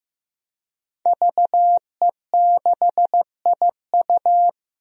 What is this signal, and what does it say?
Morse code, heard as a keyed tone, about 15 wpm, 700 Hz. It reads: VE6IU
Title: DXCC-100 Call Signs - 5 Characters - Repeated 3x - 15wpm